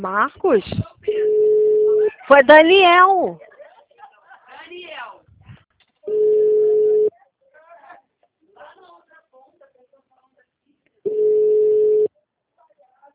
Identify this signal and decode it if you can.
Ringback tone (Europe, 425 Hz), with speech.